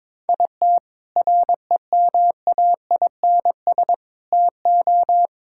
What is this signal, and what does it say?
Morse code, heard as a keyed tone, about 22 wpm, 700 Hz. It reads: IT REMAINS TO
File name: practice / 22wpm / Sets of 6 Words - 22wpm